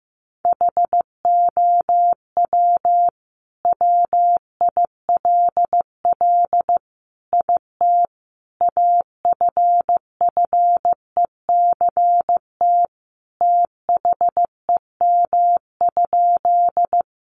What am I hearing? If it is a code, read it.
Morse code, heard as a keyed tone, about 15 wpm, 700 Hz. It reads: HOW WILL IT AFFECT THEM?
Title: Sentences from Top 900 Words 15wpm